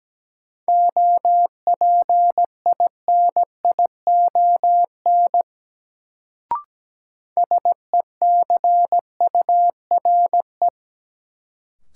Morse code, heard as a keyed tone, about 17 wpm, 700 Hz. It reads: OPINION SECURE
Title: Sentences from Top 1200 Words - New Word Drill - 17wpm